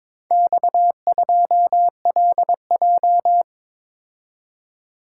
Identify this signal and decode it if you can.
Morse code, heard as a keyed tone, about 22 wpm, 700 Hz. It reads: X2LJ
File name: practice / 22wpm / Four Letter-Number Combinations - 22wpm